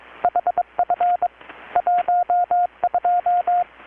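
Morse code, heard as a keyed tone, about 23 wpm, 720 Hz, with radio noise. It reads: HF 12
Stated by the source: Morse from KPH's Night of Nights XX maritime radiotelegraphy event, received in Chiapas, Mexico about 3,800 km away on a Kenwood R-600 with a grounded 10 m long-wire antenna.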